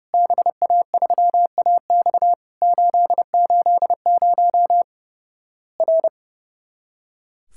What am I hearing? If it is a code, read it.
Morse code, heard as a keyed tone, about 32 wpm, 700 Hz. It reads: BA3AX 880 R
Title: Worked All Europe DX Contest - QTC Exchange - 30wpm